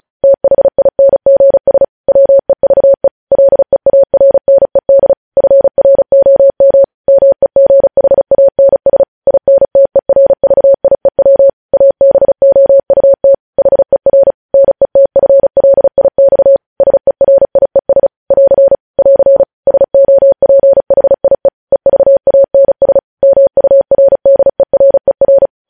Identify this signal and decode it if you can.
Morse code, heard as a keyed tone, about 36 wpm, 550 Hz. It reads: THINGS WEVE LEARNED FROM MEGHANS INTERVIEW ABOUT HER NETFLIX SERIES <AR> <AR> SOPHIE EVANS MURDERER
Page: News Headlines in Morse Code at 35 WPM